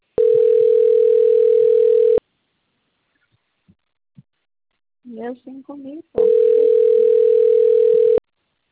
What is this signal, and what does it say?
Ringback tone (North America), 440 + 480 Hz, with speech.
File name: out- - - - - .76555